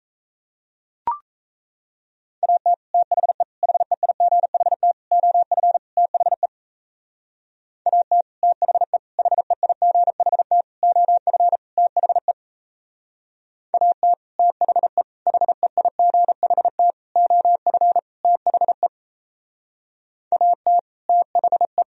Morse code, heard as a keyed tone, about 36 wpm, 700 Hz. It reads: AT THE HEIGHT OF THE AT THE HEIGHT OF THE AT THE HEIGHT OF THE AT THE